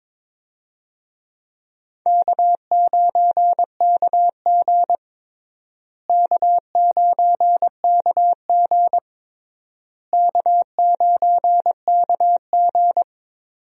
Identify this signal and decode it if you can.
Morse code, heard as a keyed tone, about 22 wpm, 700 Hz. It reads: K9KG K9KG K9KG
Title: USA Call Signs - 4 Characters - Repeated 3x - 22wpm